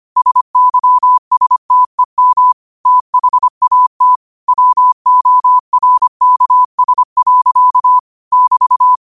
Morse code, heard as a keyed tone, about 25 wpm, 1000 Hz. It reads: SYSTEM THAT WORKS. <BT>
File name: qotdfiles / 25wpm